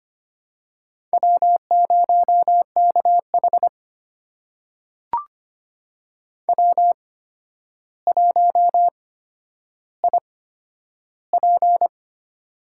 Morse code, heard as a keyed tone, about 25 wpm, spaced as for 11 wpm, 700 Hz. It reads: W0KH W1IP
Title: USA Call Signs - 4 Characters - 8x Character Spacing - 25wpm